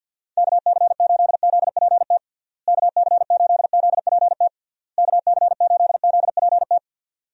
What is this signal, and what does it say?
Morse code, heard as a keyed tone, about 51 wpm, 700 Hz. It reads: KC8ZPT KC8ZPT KC8ZPT
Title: USA Call Signs - 6 Characters - Repeated 3x - 50wpm